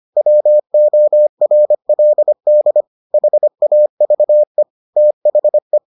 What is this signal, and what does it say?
Morse code, heard as a keyed tone, about 25 wpm, 600 Hz. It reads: WORLD HAVE THE